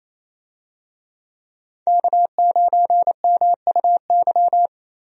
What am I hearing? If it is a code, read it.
Morse code, heard as a keyed tone, about 28 wpm, 700 Hz. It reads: K9MUY
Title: USA Call Signs - 5 Characters - Repeated 3x - 28wpm